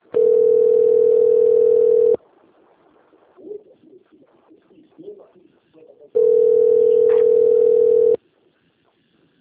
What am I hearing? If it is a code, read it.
Ringback tone (North America), 440 + 480 Hz, with speech.
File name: rg-1600-unknown- - - .16854